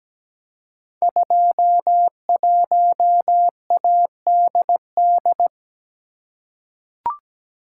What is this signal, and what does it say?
Morse code, heard as a keyed tone, about 17 wpm, 700 Hz. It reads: 21ADD
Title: Code Groups - 5 Letter:Numbers - 17wpm